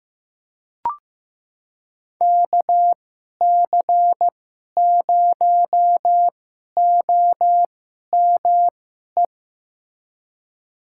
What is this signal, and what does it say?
Morse code, heard as a keyed tone, about 15 wpm, spaced as for 11 wpm, 700 Hz. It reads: KC0OME
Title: USA Call Signs - 6 Characters - 2x Character Spacing - 15wpm